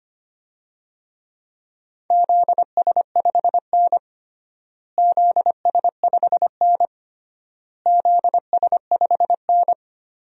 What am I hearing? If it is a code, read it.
Morse code, heard as a keyed tone, about 25 wpm, 700 Hz. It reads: ZS5N ZS5N ZS5N
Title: DXCC-100 Call Signs - 4 Characters - Repeated 3x - 25wpm